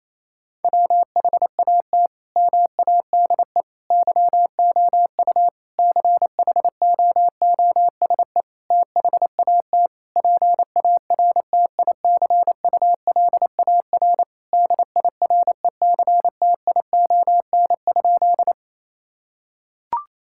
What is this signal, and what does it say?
Morse code, heard as a keyed tone, about 27 wpm, 700 Hz. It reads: WHAT MADE YOU CHOOSE THAT PARTICULAR DIRECTION?
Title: Sentences from Top 1000 Words - Review Part 5 - 28wpm